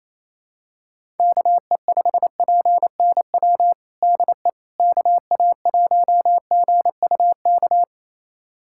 Morse code, heard as a keyed tone, about 28 wpm, 700 Hz. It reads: KE5PNW DE KA1GUK
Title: USA Callsign DE Callsign - 6 Characters - 28wpm